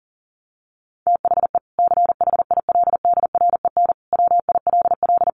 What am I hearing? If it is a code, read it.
Morse code, heard as a keyed tone, about 37 wpm, 700 Hz. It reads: THE CHILDREN WILL
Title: Sentences from Top 300 Words 40wpm